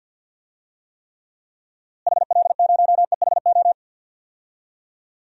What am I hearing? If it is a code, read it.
Morse code, heard as a keyed tone, about 52 wpm, 700 Hz. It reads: HL0EHO
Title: DXCC-100 Call Signs - 6 Characters - Repeated 3x - 50wpm